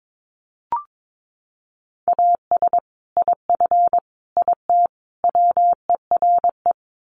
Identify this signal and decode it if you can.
Morse code, heard as a keyed tone, about 22 wpm, 700 Hz. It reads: AS IF IT WERE
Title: Sets of 4 Words 22wpm